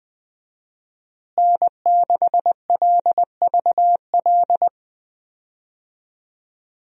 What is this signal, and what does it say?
Morse code, heard as a keyed tone, about 21 wpm, 700 Hz. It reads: N6LVL
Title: USA Call Signs - 5 Characters - Repeated 2x - 20wpm